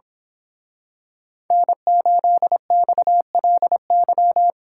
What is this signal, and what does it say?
Morse code, heard as a keyed tone, about 26 wpm, 700 Hz. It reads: N8XLY
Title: Speed Racing - USA 1x3 Call Signs 22wpm